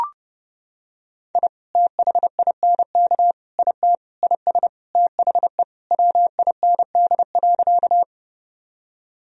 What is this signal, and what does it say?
Morse code, heard as a keyed tone, about 30 wpm, 700 Hz. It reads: I THINK IT IS THE WIND.
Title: Sentences from Top 1000 Words - Review Part 2 - 30wpm